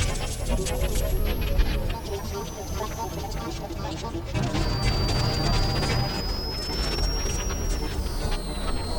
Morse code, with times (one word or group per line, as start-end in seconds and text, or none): none